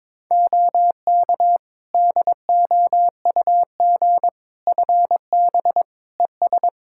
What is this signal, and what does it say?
Morse code, heard as a keyed tone, about 22 wpm, 700 Hz. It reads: OK DOUG FB ES
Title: Practice QSO - Exchange 2 - Info - 22wpm